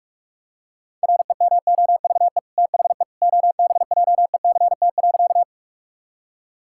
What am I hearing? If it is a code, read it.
Morse code, heard as a keyed tone, about 41 wpm, 700 Hz. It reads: REMOVE THE OBJECT.
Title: Sentences from Top 1000 Words - Review Part 5 - 45wpm